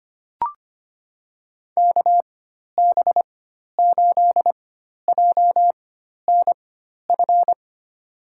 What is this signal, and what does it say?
Morse code, heard as a keyed tone, about 25 wpm, spaced as for 12 wpm, 700 Hz. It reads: KB8JNF
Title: USA Call Signs - 6 Characters - 4x Character Spacing - 25wpm